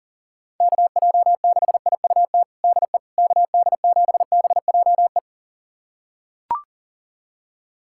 Morse code, heard as a keyed tone, about 40 wpm, 700 Hz. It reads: KJ6IUT DE KD7BJE
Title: USA Callsign DE Callsign 40wpm